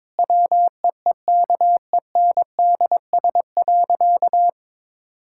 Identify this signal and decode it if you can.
Morse code, heard as a keyed tone, about 22 wpm, 700 Hz. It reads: WEEKENDS.
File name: Difficult Sentences with Hints - 22wpm